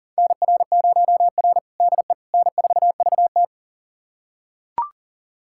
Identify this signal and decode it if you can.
Morse code, heard as a keyed tone, about 40 wpm, 700 Hz. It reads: NR0R DE N4VT
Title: USA Callsign DE Callsign - 4 Characters - 40wpm